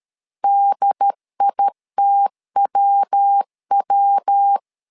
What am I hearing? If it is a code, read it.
Morse code, heard as a keyed tone, about 13 wpm, 800 Hz. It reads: DITWW
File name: LEZIONI / 3 MWRO / 3 MW 12wpm 1633